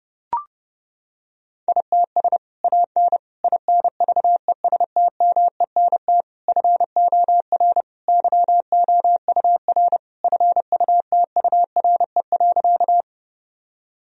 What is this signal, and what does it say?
Morse code, heard as a keyed tone, about 30 wpm, 700 Hz. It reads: ITS AN INVESTMENT FOR YOUR FUTURE.
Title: Sentences from Top 1000 Words - Review Part 4 - 30wpm